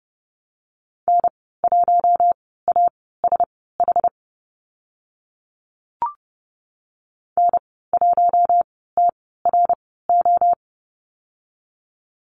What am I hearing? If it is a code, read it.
Morse code, heard as a keyed tone, about 30 wpm, spaced as for 17 wpm, 700 Hz. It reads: N1ASH N1TRO
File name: Lesson 14g - Number 2 - Call Signs 30wpm